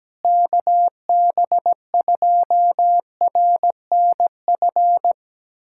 Morse code, heard as a keyed tone, about 17 wpm, 700 Hz. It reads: KB2RNF
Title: USA Callsign DE Callsign - 6 Characters - 17wpm